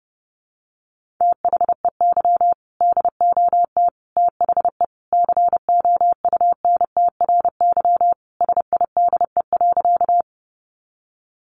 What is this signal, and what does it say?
Morse code, heard as a keyed tone, about 30 wpm, 700 Hz. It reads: THEY DOT THE COUNTRY SIDE.
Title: Sentences from Top 1500 Words 30wpm